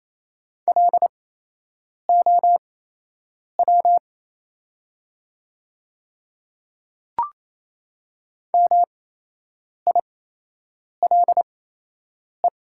Morse code, heard as a keyed tone, about 28 wpm, spaced as for 8 wpm, 690 Hz. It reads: LOW MILE
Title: 8x Character Spacing - Top 300 Words - 28wpm